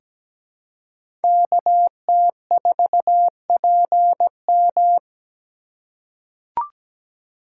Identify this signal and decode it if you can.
Morse code, heard as a keyed tone, about 17 wpm, 700 Hz. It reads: KT4PM
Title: USA Call Signs - 3x Character Spacing - 17wpm